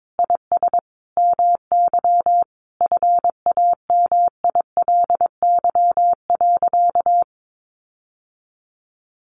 Morse code, heard as a keyed tone, about 22 wpm, 700 Hz. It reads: IS MY FAMILY.
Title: Sentences from Top 1500 Words 22wpm